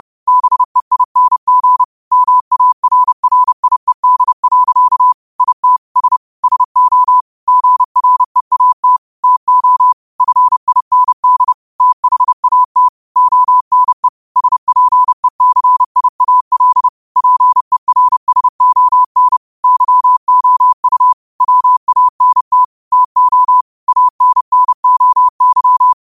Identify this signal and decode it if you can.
Morse code, heard as a keyed tone, about 30 wpm, 1000 Hz. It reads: BEING MARRIED. IT S SO GREAT TO FIND THAT ONE SPECIAL PERSON YOU WANT TO ANNOY